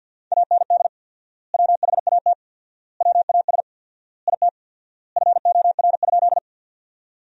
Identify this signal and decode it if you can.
Morse code, heard as a keyed tone, about 50 wpm, 700 Hz. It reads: AND WHAT WAS IT FOR?
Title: 4x Word-Spacing - Sentences from Top 100 Words - 50wpm